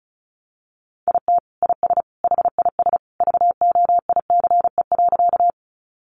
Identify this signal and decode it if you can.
Morse code, heard as a keyed tone, about 34 wpm, 700 Hz. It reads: IT IS HIS VOICE.
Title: Sentences from Top 500 Words 35wpm